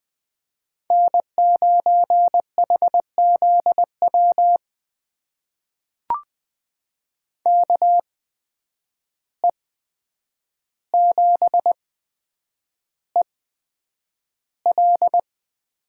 Morse code, heard as a keyed tone, about 20 wpm, spaced as for 9 wpm, 700 Hz. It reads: N9HZW KE7EL
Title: USA Call Signs - 5 Characters - 8x Character Spacing - 20wpm